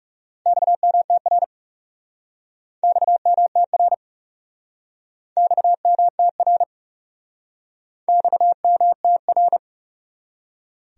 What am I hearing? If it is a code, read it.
Morse code, heard as a keyed tone, about 40 wpm, 700 Hz. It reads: XMTR XMTR XMTR XMTR